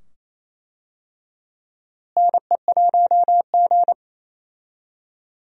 Morse code, heard as a keyed tone, about 28 wpm, 700 Hz. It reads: NE1G
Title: USA Call Signs - 4 Characters - Repeated 2x - 28wpm